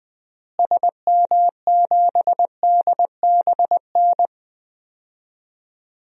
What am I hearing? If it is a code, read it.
Morse code, heard as a keyed tone, about 20 wpm, 700 Hz. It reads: SM7DBN
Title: DXCC-100 Call Signs - 6 Characters - Repeated 3x - 20wpm